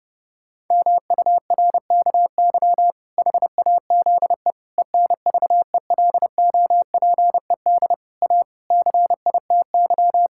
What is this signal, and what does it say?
Morse code, heard as a keyed tone, about 31 wpm, 700 Hz. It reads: MURKY HAZE ENVELOPED A CITY
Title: Pangram Challenge v2 - 30wpm